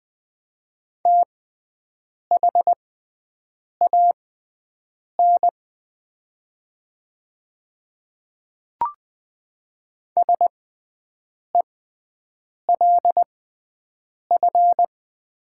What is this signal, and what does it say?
Morse code, heard as a keyed tone, about 20 wpm, spaced as for 7 wpm, 700 Hz. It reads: THAN SELF
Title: 6x Character Spacing - Top 300 Words - 20wpm